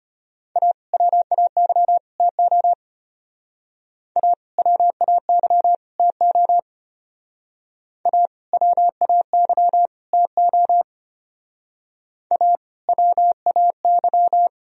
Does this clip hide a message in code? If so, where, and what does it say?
Yes: Morse code from 0.6 to 14.5 s, heard as a keyed tone, about 33 wpm, 700 Hz. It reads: A WAY TO A WAY TO A WAY TO A WAY